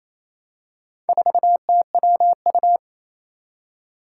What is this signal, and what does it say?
Morse code, heard as a keyed tone, about 28 wpm, 700 Hz. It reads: 4TWU
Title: Four Challenging Letter-Number Combinations 28wpm